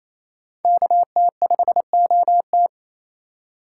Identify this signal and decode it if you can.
Morse code, heard as a keyed tone, about 37 wpm, 700 Hz. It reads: KT5OT